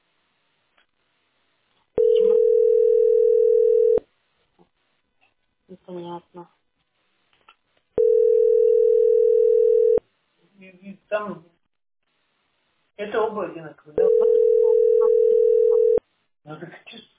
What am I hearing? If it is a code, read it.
Ringback tone (North America), 440 + 480 Hz, with speech.
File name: 4b -9122-4cd3-bad4